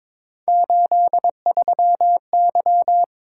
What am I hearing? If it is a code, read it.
Morse code, heard as a keyed tone, about 26 wpm, 700 Hz. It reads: 83Y
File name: Speed-Racing - Three Letter-Number Combinations - 22wpm